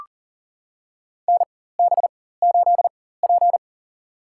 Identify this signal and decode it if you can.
Morse code, heard as a keyed tone, about 39 wpm, spaced as for 19 wpm, 700 Hz. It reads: NB8P
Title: USA Call Signs - 4 Characters - 4x Character Spacing - 40wpm